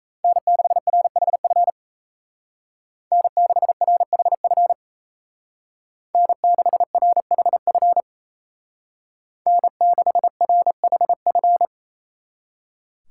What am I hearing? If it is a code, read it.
Morse code, heard as a keyed tone, about 34 wpm, 700 Hz. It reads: N6RHF N6RHF N6RHF N6RHF
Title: Speed Racing - USA 1x3 Call Signs 28wpm